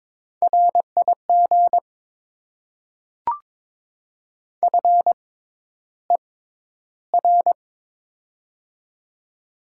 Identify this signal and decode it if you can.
Morse code, heard as a keyed tone, about 22 wpm, spaced as for 11 wpm, 690 Hz. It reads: RIG FER
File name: Top 20 QSO Elements - 6x Character Spacing - 22wpm